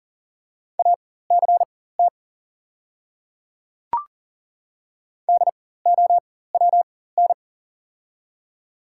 Morse code, heard as a keyed tone, about 40 wpm, spaced as for 19 wpm, 700 Hz. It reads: ACT DOWN
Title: 4x Character Spacing - Top 200 Words - 40wpm